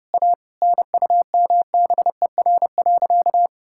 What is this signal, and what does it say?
Morse code, heard as a keyed tone, about 30 wpm, 700 Hz. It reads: A NUMBER.
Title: Sentences from Top 1000 Words - Review Part 6 - 30wpm